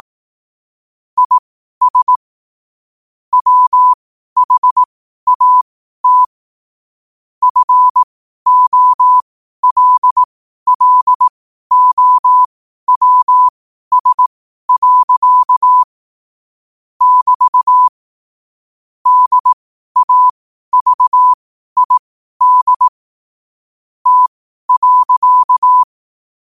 Morse code, heard as a keyed tone, about 18 wpm, spaced as for 12 wpm, 1000 Hz. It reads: IS WHAT FOLLOWS. <BT> DAVID T.
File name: qotdfiles / 12wpm